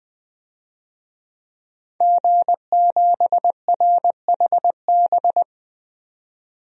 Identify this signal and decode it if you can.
Morse code, heard as a keyed tone, about 21 wpm, 700 Hz. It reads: G7RHB
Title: DXCC-100 Call Signs - 5 Characters - 20wpm